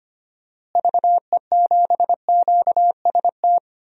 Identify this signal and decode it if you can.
Morse code, heard as a keyed tone, about 25 wpm, 700 Hz. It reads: VE7QST
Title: DXCC-100 Call Signs - 6 Characters - Repeated 3x - 25wpm